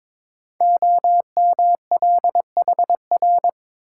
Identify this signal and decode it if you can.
Morse code, heard as a keyed tone, about 21 wpm, 700 Hz. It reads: OMLHR
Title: Code Groups - 5 Letters - 22wpm